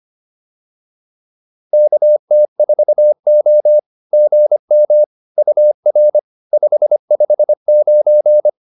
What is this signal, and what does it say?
Morse code, heard as a keyed tone, about 25 wpm, 600 Hz. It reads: KT4O GM UR 559